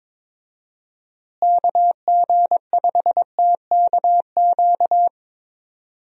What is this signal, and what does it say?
Morse code, heard as a keyed tone, about 22 wpm, 700 Hz. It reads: KG5TKQ
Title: USA Call Signs - 6 Characters - Repeated 2x - 22wpm